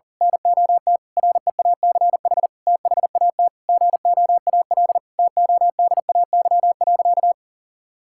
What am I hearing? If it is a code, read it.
Morse code, heard as a keyed tone, about 37 wpm, 700 Hz. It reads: NOT REACH THAT GOAL TODAY.
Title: Sentences from Top 1000 Words - Review Part 5 - 40wpm